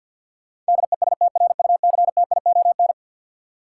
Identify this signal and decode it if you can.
Morse code, heard as a keyed tone, about 53 wpm, 700 Hz. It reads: DESTRUCTION
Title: Sets of 5 Words Encore - New Word Drill - 50wpm